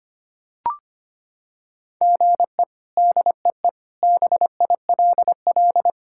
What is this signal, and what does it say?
Morse code, heard as a keyed tone, about 25 wpm, 700 Hz. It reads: GE DEE BILL
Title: K1USN SST QSO Practice - 25wpm